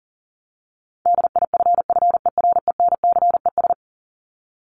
Morse code, heard as a keyed tone, about 40 wpm, 700 Hz. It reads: DIFFERENCES
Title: ES:S Suffix - Top 100 Words with Base Words 40wpm